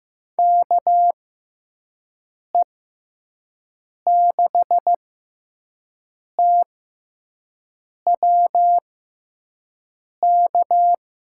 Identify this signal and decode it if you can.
Morse code, heard as a keyed tone, about 15 wpm, spaced as for 5 wpm, 700 Hz. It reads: KE6TW K
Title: USA Call Signs - 6x Character Spacing - 15wpm